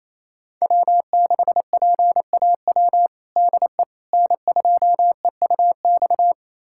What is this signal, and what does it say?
Morse code, heard as a keyed tone, about 28 wpm, 700 Hz. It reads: W6PAW DE N2EUX
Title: USA Callsign DE Callsign - 5 Characters - 28wpm